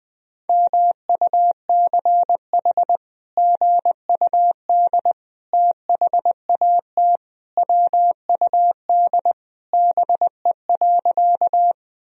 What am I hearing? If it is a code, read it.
Morse code, heard as a keyed tone, about 20 wpm, 700 Hz. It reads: MUCH GUD THAT WUD BE.
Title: Sentences from Top 200 Words and Abbreviations 20wpm